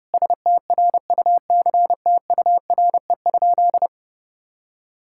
Morse code, heard as a keyed tone, about 32 wpm, 700 Hz. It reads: STRUCTURE?
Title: Sentences from Top 1000 Words - Review Part 6 - 30wpm